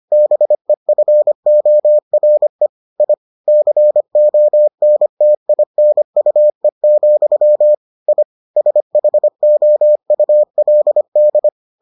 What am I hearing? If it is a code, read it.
Morse code, heard as a keyed tone, about 25 wpm, 600 Hz. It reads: BEFORE I CONTINUE, I SHOULD